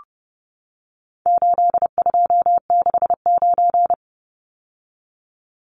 Morse code, heard as a keyed tone, about 30 wpm, 700 Hz. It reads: 8269
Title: Numbers - 4-Digit 30wpm